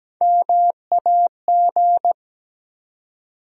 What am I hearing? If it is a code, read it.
Morse code, heard as a keyed tone, about 17 wpm, 700 Hz. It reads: MAG
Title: USA Call Sign Postfixes - 3 Letters 17wpm